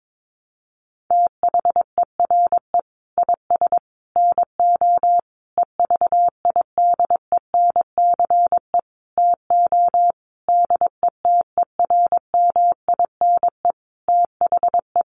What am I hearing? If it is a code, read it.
Morse code, heard as a keyed tone, about 22 wpm, 700 Hz. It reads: THERE IS NO EVIDENCE TO DETERMINE THE
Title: Sentences from Top 1300 Words 22wpm